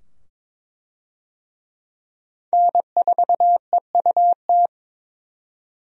Morse code, heard as a keyed tone, about 21 wpm, 700 Hz. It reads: N4EUT
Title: USA Call Signs - 5 Characters - 22wpm